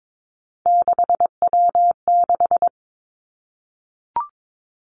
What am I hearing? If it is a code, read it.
Morse code, heard as a keyed tone, about 22 wpm, 700 Hz. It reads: 6W6
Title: Three Challenging Letter-Number Combinations 22wpm